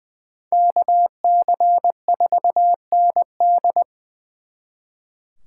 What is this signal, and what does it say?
Morse code, heard as a keyed tone, about 23 wpm, 700 Hz. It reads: KC4ND
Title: Speed Racing - USA 2x2 Call Signs 20wpm